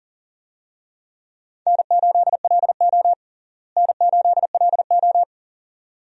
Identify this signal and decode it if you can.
Morse code, heard as a keyed tone, about 40 wpm, 700 Hz. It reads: N8LO N8LO
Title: USA Call Signs - 4 Characters - Repeated 2x - 40wpm